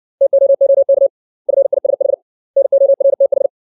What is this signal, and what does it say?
Morse code, heard as a keyed tone, about 60 wpm, 550 Hz. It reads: TOOK FISH NORTH
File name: 632 words / 632 Words 60WPM